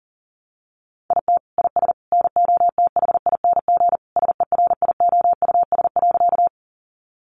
Morse code, heard as a keyed tone, about 40 wpm, 700 Hz. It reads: IT IS NOTHING SERIOUS.
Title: Sentences from Top 1100 Words 40wpm